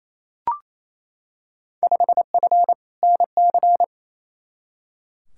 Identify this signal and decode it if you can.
Morse code, heard as a keyed tone, about 28 wpm, 700 Hz. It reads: 5F NC
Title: Field Day Exchanges Encore - 28wpm